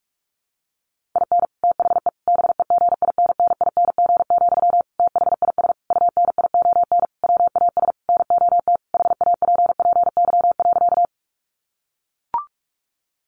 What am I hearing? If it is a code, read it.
Morse code, heard as a keyed tone, about 44 wpm, 700 Hz. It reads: IN THE BEGINNING, THIS UNION WAS NOT HAPPY.
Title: Sentences from Top 1400 Words 45wpm